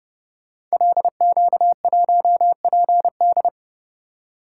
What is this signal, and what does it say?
Morse code, heard as a keyed tone, about 28 wpm, 700 Hz. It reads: LQ1PD
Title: Code Groups - 5 Letter:Numbers - 30wpm